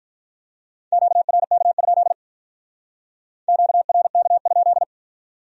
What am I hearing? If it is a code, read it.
Morse code, heard as a keyed tone, about 43 wpm, 700 Hz. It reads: QRK? QRK?
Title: Speed-Racing Q-Codes - 35wpm